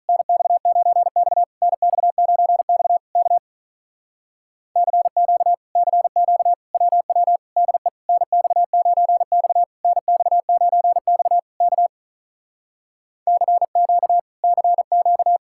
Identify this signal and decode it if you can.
Morse code, heard as a keyed tone, about 46 wpm, 700 Hz. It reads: NX9X NX9X K CQ CQ WW DE NX9X NX9X K CQ CQ